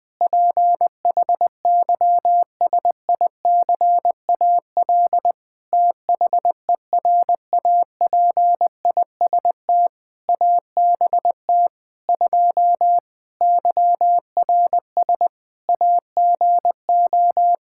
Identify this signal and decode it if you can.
Morse code, heard as a keyed tone, about 20 wpm, 700 Hz. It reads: PHYSICAL THERAPIST ABT 2 YRS AGO
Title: K1USN SST QSO Practice 20wpm